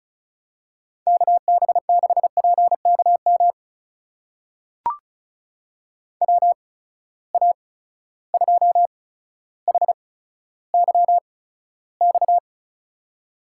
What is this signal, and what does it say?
Morse code, heard as a keyed tone, about 36 wpm, spaced as for 15 wpm, 700 Hz. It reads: KB6PKM WA2HYX
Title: USA Call Signs - 6 Characters - 8x Character Spacing - 35wpm